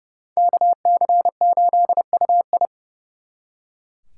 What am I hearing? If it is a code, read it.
Morse code, heard as a keyed tone, about 38 wpm, 700 Hz. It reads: KC8UI